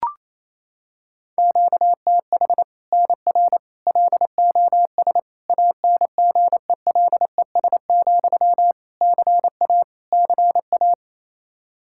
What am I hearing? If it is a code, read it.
Morse code, heard as a keyed tone, about 28 wpm, 700 Hz. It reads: QTH NR LOS ANGELES, CA CA
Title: Practice QSO - Exchange 1 - Beginning - 28wpm